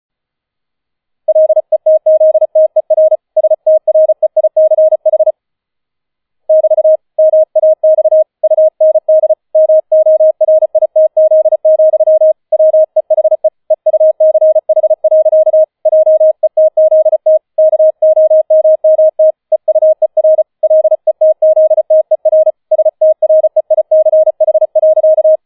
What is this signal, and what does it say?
Morse code, heard as a keyed tone, about 34 wpm, 630 Hz. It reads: LETZTER STREICH <BT> MAX UND MORITZ, WEHE EUCH. JETZT KOMMT EUER LETZTER STREICH.